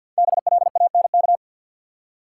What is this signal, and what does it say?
Morse code, heard as a keyed tone, about 50 wpm, 700 Hz. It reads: BLANK
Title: Five Letter Words - 50wpm